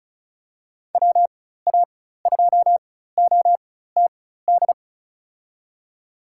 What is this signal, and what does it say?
Morse code, heard as a keyed tone, about 36 wpm, spaced as for 16 wpm, 700 Hz. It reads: WA2OTD
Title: USA Call Signs - 4x Character Spacing - 35wpm